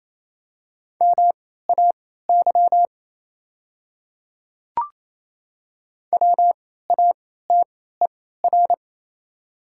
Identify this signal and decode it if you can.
Morse code, heard as a keyed tone, about 28 wpm, spaced as for 16 wpm, 700 Hz. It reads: MAY WATER
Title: 3x Character Spacing - Top 100 Words - 28wpm